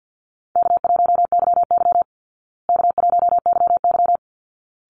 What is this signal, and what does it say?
Morse code, heard as a keyed tone, about 50 wpm, 700 Hz. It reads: K1YY K1YY
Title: Easy Call Signs Repeated 2x - 50wpm